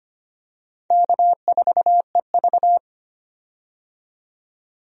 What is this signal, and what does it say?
Morse code, heard as a keyed tone, about 25 wpm, 700 Hz. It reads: K4EV
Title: USA Call Signs - 4 Characters - 25wpm